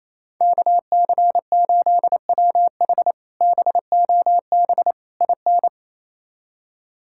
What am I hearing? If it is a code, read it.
Morse code, heard as a keyed tone, about 28 wpm, 700 Hz. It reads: KC8WH BOB IN
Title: K1USN SST QSO Practice - 28wpm